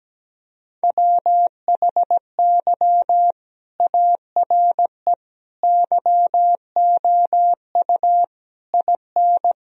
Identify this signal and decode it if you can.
Morse code, heard as a keyed tone, about 17 wpm, 700 Hz. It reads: WHY ARE YOU IN
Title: Sentences from Top 1000 Words - Review Part 2 - 17wpm